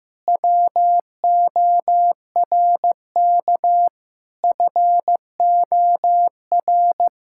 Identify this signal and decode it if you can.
Morse code, heard as a keyed tone, about 15 wpm, 700 Hz. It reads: WORK FOR
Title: Pangram Challenge v2 - 15wpm